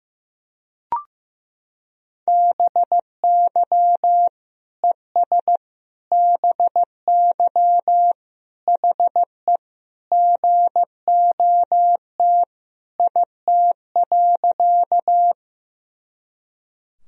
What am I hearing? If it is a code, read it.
Morse code, heard as a keyed tone, about 15 wpm, 700 Hz. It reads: BY ES BY HE GOT IT.
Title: Sentences from Top 300 Words and Abbreviations 15wpm